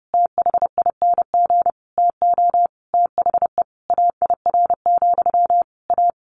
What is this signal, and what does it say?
Morse code, heard as a keyed tone, about 30 wpm, 700 Hz. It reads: THING TO THE AIR, A
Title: A Sound Of Thunder 30wpm